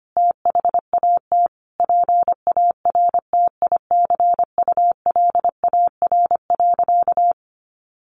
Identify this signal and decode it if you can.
Morse code, heard as a keyed tone, about 25 wpm, 700 Hz. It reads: THAT PARTICULAR.
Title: Sentences from Top 600 Words 25wpm